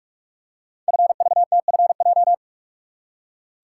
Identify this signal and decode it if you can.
Morse code, heard as a keyed tone, about 44 wpm, 700 Hz. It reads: FVTFJ
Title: Code Groups - 5 Letters - 45wpm